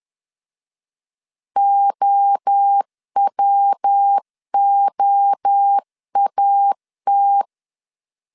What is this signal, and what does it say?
Morse code, heard as a keyed tone, about 10 wpm, 800 Hz. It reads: OWOAT
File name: 10wpm